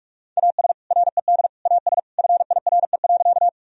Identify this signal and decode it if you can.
Morse code, heard as a keyed tone, about 43 wpm, 700 Hz. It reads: AS RED AS FIRE.
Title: Sentences from Top 1000 Words - Review Part 2 - 45wpm